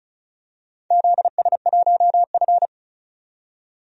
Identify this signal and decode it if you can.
Morse code, heard as a keyed tone, about 36 wpm, 700 Hz. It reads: ZS1F
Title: DXCC-100 Call Signs - 4 Characters - 35wpm